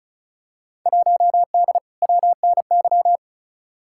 Morse code, heard as a keyed tone, about 36 wpm, 700 Hz. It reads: 1D WNY
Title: Field Day Exchanges - 35wpm